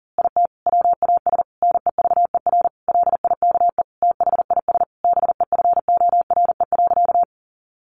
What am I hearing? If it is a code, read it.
Morse code, heard as a keyed tone, about 40 wpm, 700 Hz. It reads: IT WAS NEVER LIKE THIS BEFORE.
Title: A Sound Of Thunder 40wpm